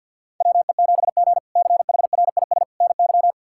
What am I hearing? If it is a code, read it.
Morse code, heard as a keyed tone, about 52 wpm, 690 Hz. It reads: WE7G CHRIS NY
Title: K1USN SST QSO Practice - 50wpm